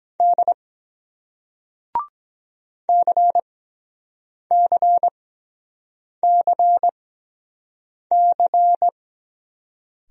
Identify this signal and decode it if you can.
Morse code, heard as a keyed tone, about 23 wpm, spaced as for 7 wpm, 700 Hz. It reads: D CCCC